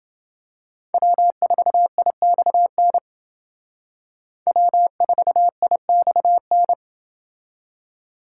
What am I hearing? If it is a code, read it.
Morse code, heard as a keyed tone, about 29 wpm, 700 Hz. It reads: W4IXN W4IXN